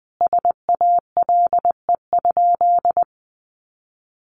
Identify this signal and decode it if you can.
Morse code, heard as a keyed tone, about 20 wpm, 700 Hz. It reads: SALE?
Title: Sentences from Top 1400 Words 20wpm